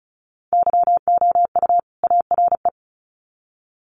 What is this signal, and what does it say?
Morse code, heard as a keyed tone, about 34 wpm, 700 Hz. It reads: YOU ARE
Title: Sets of 2 Words 35wpm